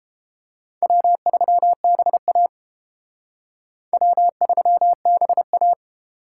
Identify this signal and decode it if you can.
Morse code, heard as a keyed tone, about 29 wpm, 700 Hz. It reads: W3BA W3BA